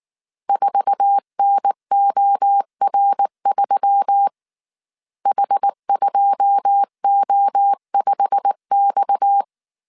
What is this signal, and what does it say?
Morse code, heard as a keyed tone, about 19 wpm, 800 Hz. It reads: 4NOR3 H2O5X
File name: CW / DET / 1 9 1500C S A 18wpm 2459